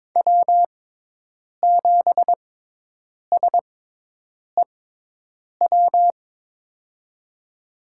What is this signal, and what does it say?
Morse code, heard as a keyed tone, about 23 wpm, spaced as for 8 wpm, 700 Hz. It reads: W7SEW
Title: USA Call Signs - 6x Character Spacing - 22wpm